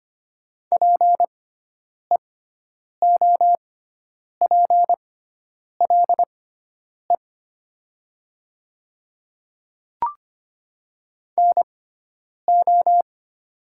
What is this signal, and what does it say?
Morse code, heard as a keyed tone, about 25 wpm, spaced as for 9 wpm, 700 Hz. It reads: PEOPLE NO
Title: 6x Character Spacing - Top 100 Words - 25wpm